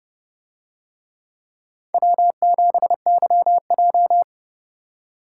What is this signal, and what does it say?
Morse code, heard as a keyed tone, about 28 wpm, 700 Hz. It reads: W7YJ